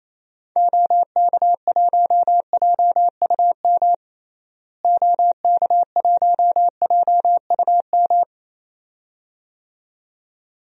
Morse code, heard as a keyed tone, about 28 wpm, 700 Hz. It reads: OK1JUM OK1JUM
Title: DXCC-100 Call Signs - 6 Characters - Repeated 2x - 28wpm